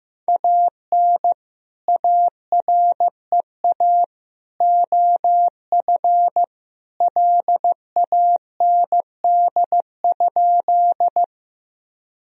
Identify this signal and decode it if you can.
Morse code, heard as a keyed tone, about 15 wpm, 700 Hz. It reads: AN AREA OF LAND?
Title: Sentences from Top 1000 Words - Review Part 2 - 15wpm